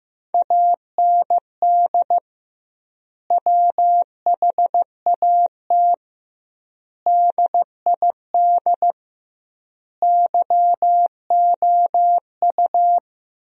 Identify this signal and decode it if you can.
Morse code, heard as a keyed tone, about 15 wpm, 700 Hz. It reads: AND WHAT DID YOU
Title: 2x Word-Spacing - Sentences from Top 300 Words - 15wpm